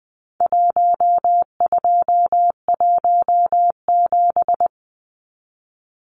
Morse code, heard as a keyed tone, about 20 wpm, 700 Hz. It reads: 1217
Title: Numbers - Contest Style - 4-Digit 20wpm